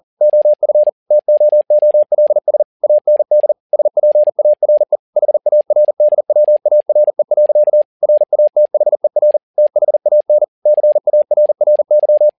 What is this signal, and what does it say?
Morse code, heard as a keyed tone, about 40 wpm, 600 Hz. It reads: OF TOOLS AND SPARE HARDWARE. RATHER THAN CARRY
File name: learning center / tower trouble / tower touble 40wpm